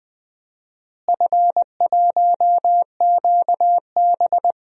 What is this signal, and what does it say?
Morse code, heard as a keyed tone, about 20 wpm, 700 Hz. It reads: F1QB
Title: DXCC-100 Call Signs - 4 Characters - Repeated 2x - 20wpm